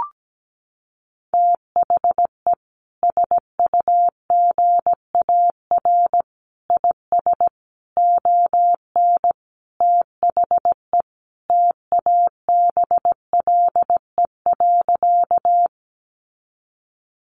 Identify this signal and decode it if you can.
Morse code, heard as a keyed tone, about 17 wpm, 700 Hz. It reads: THE SUGAR IS ON THE TABLE.
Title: Sentences from Top 1200 Words 17wpm